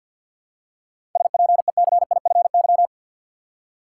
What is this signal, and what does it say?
Morse code, heard as a keyed tone, about 53 wpm, 700 Hz. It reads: SPECIFY
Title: Seven Letter Words - 50wpm